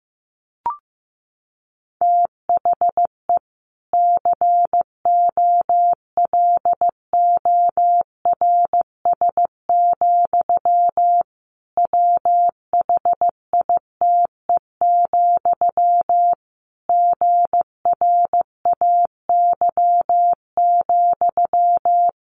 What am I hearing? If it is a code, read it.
Morse code, heard as a keyed tone, about 15 wpm, 700 Hz. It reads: THE COLORS, WHITE, GRAY,
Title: A Sound Of Thunder 15wpm